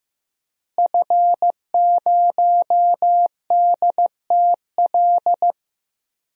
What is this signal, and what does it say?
Morse code, heard as a keyed tone, about 16 wpm, 700 Hz. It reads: F0DTL
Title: DXCC-100 Call Signs - 5 Characters - Repeated 3x - 15wpm